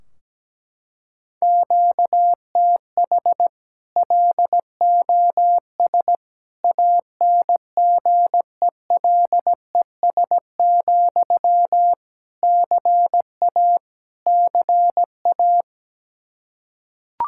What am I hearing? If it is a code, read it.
Morse code, heard as a keyed tone, about 17 wpm, 700 Hz. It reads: QTH LOS ANGELES, CA CA
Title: Practice QSO - Exchange 1 - Beginning - 17wpm